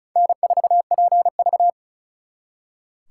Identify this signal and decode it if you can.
Morse code, heard as a keyed tone, about 45 wpm, 700 Hz. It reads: N4PV